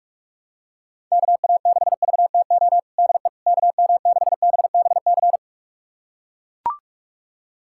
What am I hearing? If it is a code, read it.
Morse code, heard as a keyed tone, about 44 wpm, 700 Hz. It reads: KA6VTO DE KM6BBC
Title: USA Callsign DE Callsign 45wpm